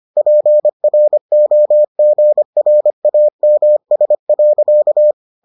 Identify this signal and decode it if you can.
Morse code, heard as a keyed tone, about 25 wpm, 600 Hz. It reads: PROGRAMS.